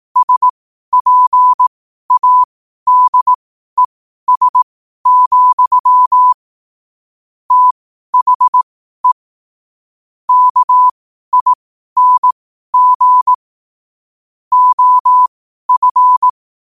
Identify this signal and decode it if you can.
Morse code, heard as a keyed tone, about 18 wpm, spaced as for 12 wpm, 1000 Hz. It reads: SPADES, THE KING OF
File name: qotdfiles / 12wpm